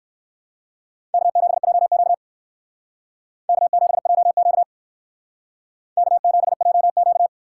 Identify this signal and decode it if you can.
Morse code, heard as a keyed tone, about 57 wpm, 700 Hz. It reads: K7JQ K7JQ K7JQ